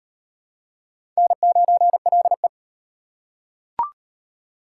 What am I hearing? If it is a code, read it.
Morse code, heard as a keyed tone, about 36 wpm, 700 Hz. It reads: N9LE